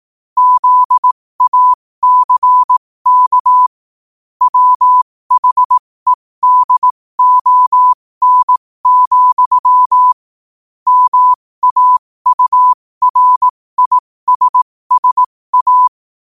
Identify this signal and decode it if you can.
Morse code, heard as a keyed tone, about 18 wpm, spaced as for 15 wpm, 1000 Hz. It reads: ZACK WHEDON, MAURISSA
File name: qotdfiles / 15wpm